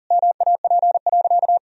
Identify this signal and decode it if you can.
Morse code, heard as a keyed tone, about 40 wpm, 700 Hz. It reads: MAP.
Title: Sentences from Top 1000 Words - Review Part 2 - 40wpm